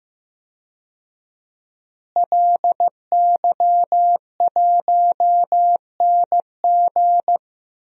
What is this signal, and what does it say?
Morse code, heard as a keyed tone, about 14 wpm, 700 Hz. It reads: LY1NG
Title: DXCC-100 Call Signs - 5 Characters - Repeated 3x - 15wpm